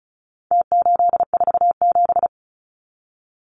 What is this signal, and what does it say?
Morse code, heard as a keyed tone, about 34 wpm, 700 Hz. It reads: T847
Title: Numbers - Contest Style - 4-Digit 35wpm